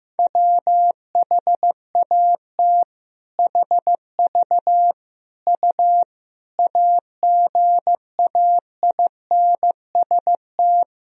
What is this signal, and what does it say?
Morse code, heard as a keyed tone, about 15 wpm, 700 Hz. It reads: WHAT HV U AGAINST
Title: Sentences from Top 400 Words and Abbreviations 15wpm